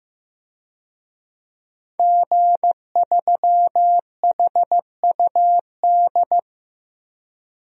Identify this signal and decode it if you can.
Morse code, heard as a keyed tone, about 15 wpm, 700 Hz. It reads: G3HUD
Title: DXCC-100 Call Signs - 5 Characters - 15wpm